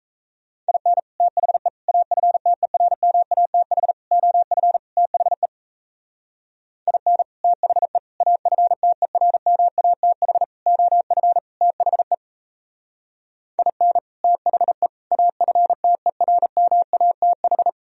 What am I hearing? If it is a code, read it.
Morse code, heard as a keyed tone, about 36 wpm, 700 Hz. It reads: IN THE AFTERMATH OF THE IN THE AFTERMATH OF THE IN THE AFTERMATH